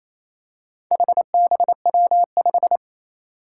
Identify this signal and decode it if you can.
Morse code, heard as a keyed tone, about 27 wpm, 700 Hz. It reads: HBW5
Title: Four Challenging Letter-Number Combinations 28wpm